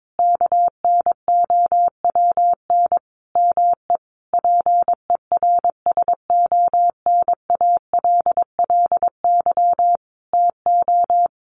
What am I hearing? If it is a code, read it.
Morse code, heard as a keyed tone, about 22 wpm, 700 Hz. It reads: KNOWN ME PERSONALLY TO
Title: Sentences from Top 1400 Words 22wpm